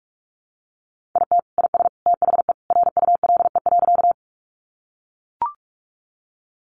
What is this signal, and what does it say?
Morse code, heard as a keyed tone, about 44 wpm, 700 Hz. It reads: IT IS THE RULE.
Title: Sentences from Top 400 Words 45wpm